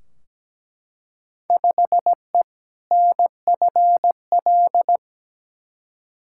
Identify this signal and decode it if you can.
Morse code, heard as a keyed tone, about 17 wpm, 700 Hz. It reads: 5E NFL
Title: Field Day Exchanges Encore - 17wpm